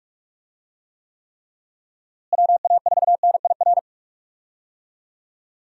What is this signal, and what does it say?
Morse code, heard as a keyed tone, about 45 wpm, 700 Hz. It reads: WA4NIR
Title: USA Call Signs - 6 Characters - Repeated 3x - 45wpm